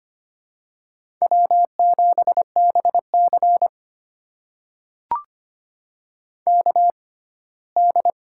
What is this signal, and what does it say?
Morse code, heard as a keyed tone, about 25 wpm, spaced as for 17 wpm, 700 Hz. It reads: W7BC KD
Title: USA Call Signs - 4 Characters - 6x Character Spacing - 25wpm